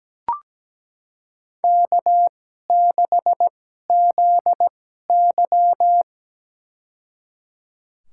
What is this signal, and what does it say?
Morse code, heard as a keyed tone, about 17 wpm, spaced as for 12 wpm, 700 Hz. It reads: K6ZY
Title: USA Call Signs - 4 Characters - 2x Character Spacing - 17wpm